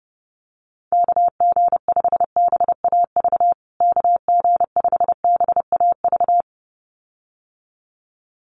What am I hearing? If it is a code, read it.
Morse code, heard as a keyed tone, about 30 wpm, 700 Hz. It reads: KG5BAV KG5BAV
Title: USA Call Signs Repeated 2x 30wpm